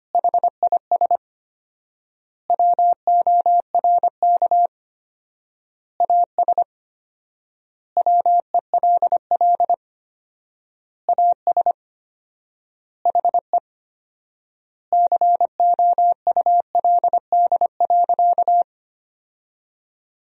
Morse code, heard as a keyed tone, about 25 wpm, 700 Hz. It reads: HIS WORK AS WELL AS HE COULD.
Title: 4x Word-Spacing - Sentences from Top 500 Words - 25wpm